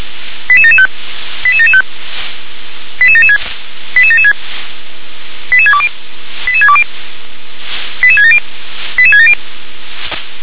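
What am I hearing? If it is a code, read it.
Five-tone selective call (ZVEI), heard as quick runs of five tones, tones of about 70 ms. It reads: 88775 88775 88776 88776 88620 88620 88670 88670
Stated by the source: ZVEI five-tone selective call received on an SDR.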